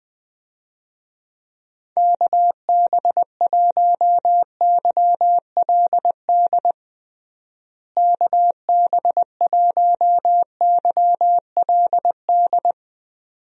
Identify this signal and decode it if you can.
Morse code, heard as a keyed tone, about 21 wpm, 700 Hz. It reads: KB1YLD KB1YLD
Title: USA Call Signs - 6 Characters - Repeated 2x - 20wpm